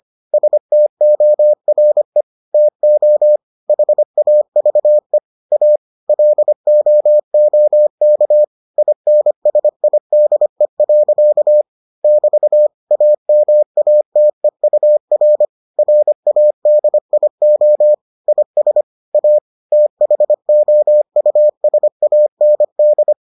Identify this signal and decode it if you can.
Morse code, heard as a keyed tone, about 25 wpm, 600 Hz. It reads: STORE TO HAVE A LOOK INSIDE. <BT> AMATEUR RADIO IS A THOUSAND